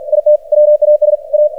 Morse code, heard as a keyed tone, about 50 wpm, 600 Hz. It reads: IT WAS A
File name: archived / CRSnet